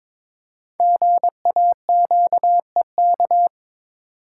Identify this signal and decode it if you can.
Morse code, heard as a keyed tone, about 22 wpm, 700 Hz. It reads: GAQEK
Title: Code Groups - 5 Letters - 22wpm